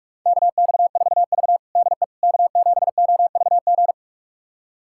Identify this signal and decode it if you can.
Morse code, heard as a keyed tone, about 46 wpm, 700 Hz. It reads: KX4V DE K7OVG
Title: USA Callsign DE Callsign 45wpm